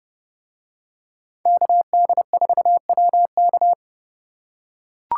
Morse code, heard as a keyed tone, about 29 wpm, 700 Hz. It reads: KD4WK